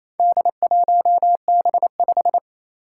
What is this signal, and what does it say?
Morse code, heard as a keyed tone, about 27 wpm, 700 Hz. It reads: D1B5
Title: Four Challenging Letter-Number Combinations 28wpm